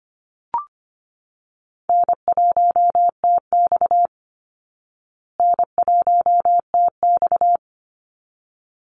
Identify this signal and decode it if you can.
Morse code, heard as a keyed tone, about 25 wpm, 700 Hz. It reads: N1TX N1TX
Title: Easy Call Signs Repeated 2x - 25wpm